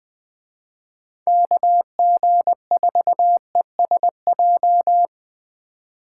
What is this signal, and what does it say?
Morse code, heard as a keyed tone, about 20 wpm, 700 Hz. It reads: KG4ESJ